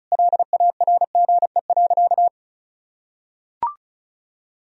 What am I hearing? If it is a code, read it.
Morse code, heard as a keyed tone, about 36 wpm, 700 Hz. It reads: LARGE.
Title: Sentences from Top 1000 Words - Review Part 1 - 35wpm